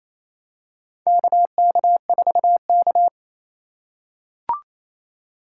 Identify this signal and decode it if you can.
Morse code, heard as a keyed tone, about 28 wpm, 700 Hz. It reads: KK4K
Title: USA Call Signs - 4 Characters - 2x Character Spacing - 28wpm